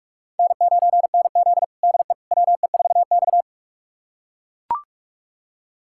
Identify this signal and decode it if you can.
Morse code, heard as a keyed tone, about 41 wpm, 700 Hz. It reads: N9NZ DE WE4X
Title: USA Callsign DE Callsign - 4 Characters - 45wpm